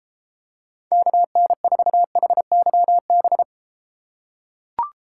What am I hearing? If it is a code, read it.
Morse code, heard as a keyed tone, about 32 wpm, 700 Hz. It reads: KN4HYB